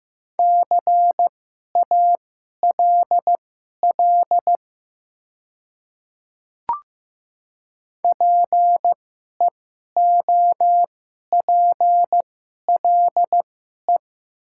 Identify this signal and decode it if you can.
Morse code, heard as a keyed tone, about 15 wpm, spaced as for 11 wpm, 700 Hz. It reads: CALL PEOPLE
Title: 2x Character Spacing - Top 100 Words - 15wpm